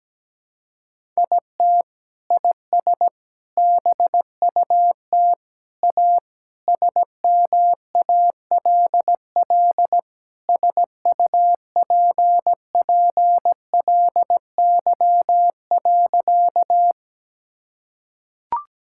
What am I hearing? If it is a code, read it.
Morse code, heard as a keyed tone, about 17 wpm, 700 Hz. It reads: IT IS BUT A SMALL SUPPLY.
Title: Sentences from Top 1000 Words - Review Part 5 - 17wpm